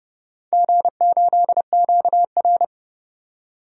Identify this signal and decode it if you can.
Morse code, heard as a keyed tone, about 30 wpm, 700 Hz. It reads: G8QR
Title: DXCC-100 Call Signs - 4 Characters - 30wpm